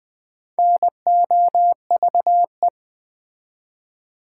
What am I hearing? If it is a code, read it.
Morse code, heard as a keyed tone, about 20 wpm, 700 Hz. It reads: NOVE
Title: Build A Word - 2 Letters At A Time - 8-Letter Words - 20wpm